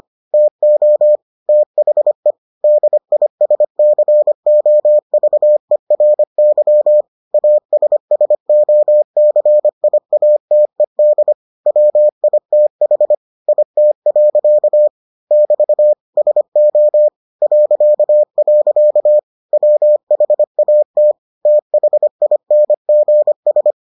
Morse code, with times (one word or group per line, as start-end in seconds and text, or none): 0.3-1.2 TO
1.5-2.3 THE
2.6-7.0 DISCOVERY
7.3-11.3 ASSOCIATED
11.7-13.1 WITH
13.5-14.9 IT.
15.3-15.9 <BT>
16.2-17.1 SO
17.4-19.2 ..
19.5-21.1 WHAT
21.5-23.7 THINGS